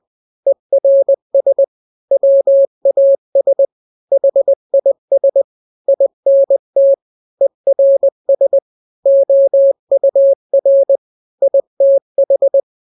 Morse code, heard as a keyed tone, about 20 wpm, 550 Hz. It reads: ERS WAS HIS INT ERS OUR ITH
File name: n-grams / 3-grams 20 18 WPM